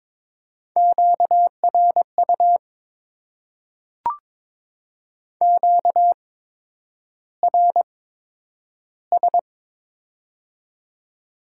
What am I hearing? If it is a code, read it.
Morse code, heard as a keyed tone, about 22 wpm, spaced as for 9 wpm, 700 Hz. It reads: QRU QRS
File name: Top 90 QSO Elements - 8x Character Spacing - 22wpm